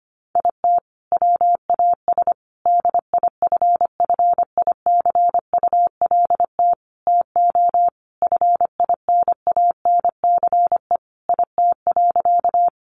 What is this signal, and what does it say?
Morse code, heard as a keyed tone, about 25 wpm, 700 Hz. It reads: IT WAS DIFFICULT TO FINANCE IT.
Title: Sentences from Top 1500 Words 25wpm